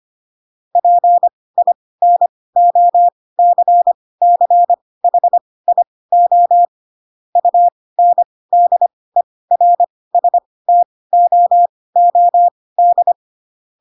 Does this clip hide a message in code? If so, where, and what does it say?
Yes: Morse code from 0.8 to 13.1 s, heard as a keyed tone, about 25 wpm, spaced as for 18 wpm, 700 Hz. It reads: PINOCCHIO UNDERSTOOD